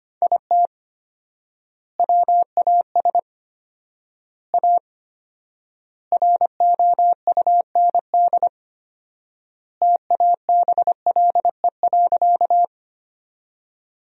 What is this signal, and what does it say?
Morse code, heard as a keyed tone, about 25 wpm, 700 Hz. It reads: IT WAS A ROUND TABLE.
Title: 4x Word-Spacing - Sentences from Top 400 Words - 25wpm